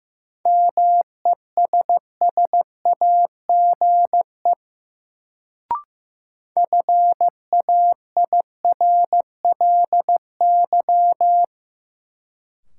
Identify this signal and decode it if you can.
Morse code, heard as a keyed tone, about 15 wpm, 700 Hz. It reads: MESSAGE FAIRLY
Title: Sentences from Top 1100 Words - New Word Drill - 15wpm